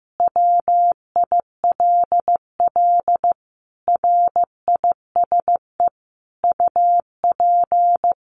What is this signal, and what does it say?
Morse code, heard as a keyed tone, about 15 wpm, 700 Hz. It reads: WILL RISE UP
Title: Sentences from Top 900 Words 15wpm